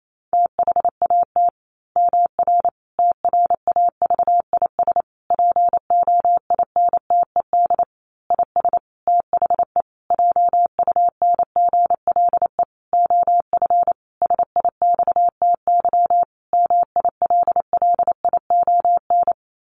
Morse code, heard as a keyed tone, about 28 wpm, 700 Hz. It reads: THAT MR TRAVIS POINTED IS THE JUNGLE OF SIXTY MILLION
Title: A Sound Of Thunder 28wpm